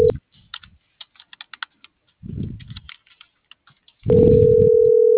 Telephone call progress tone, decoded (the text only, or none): ringback tone